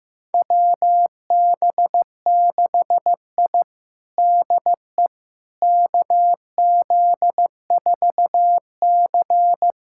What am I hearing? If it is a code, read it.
Morse code, heard as a keyed tone, about 15 wpm, 700 Hz. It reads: WB6I DE KZ4C
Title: USA Callsign DE Callsign - 4 Characters - 15wpm